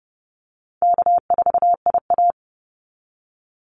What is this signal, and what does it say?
Morse code, heard as a keyed tone, about 29 wpm, 700 Hz. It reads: K4IA
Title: Easy Call Signs Repeated 3x - 30wpm